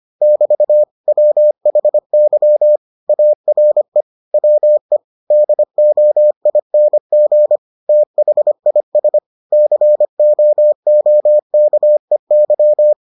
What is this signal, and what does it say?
Morse code, heard as a keyed tone, about 25 wpm, 600 Hz. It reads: <BT> WHY ARE WE DOING THIS COOKEY